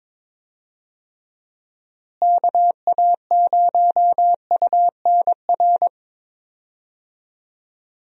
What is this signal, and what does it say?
Morse code, heard as a keyed tone, about 22 wpm, 700 Hz. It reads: KA0UNR
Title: USA Call Signs - 6 Characters - Repeated 2x - 22wpm